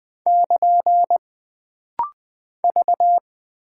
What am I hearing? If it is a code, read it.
Morse code, heard as a keyed tone, about 20 wpm, 700 Hz. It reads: <KN> V
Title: Single Character (Letter:Number:Punctuation:Prosign) - Rapid-Fire - 20wpm